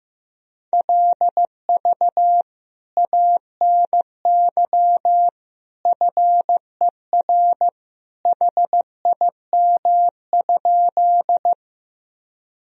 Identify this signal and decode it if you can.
Morse code, heard as a keyed tone, about 15 wpm, 700 Hz. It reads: LV ANY FER HIM?
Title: Sentences from Top 400 Words and Abbreviations 15wpm